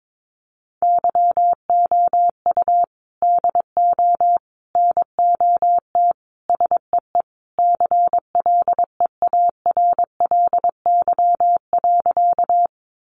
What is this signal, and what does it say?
Morse code, heard as a keyed tone, about 22 wpm, 700 Hz. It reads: YOU DO NOT SEE CLEARLY.
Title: Sentences from Top 1300 Words 22wpm